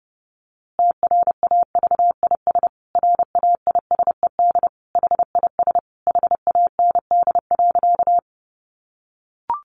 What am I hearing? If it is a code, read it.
Morse code, heard as a keyed tone, about 30 wpm, 700 Hz. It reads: TRAVIS RAISED HIS HAND.
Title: A Sound Of Thunder 30wpm